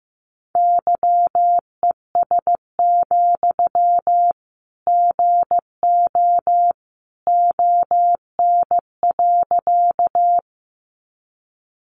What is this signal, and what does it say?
Morse code, heard as a keyed tone, about 15 wpm, 700 Hz. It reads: YES, GO ON.
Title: Sentences from Top 500 Words 15wpm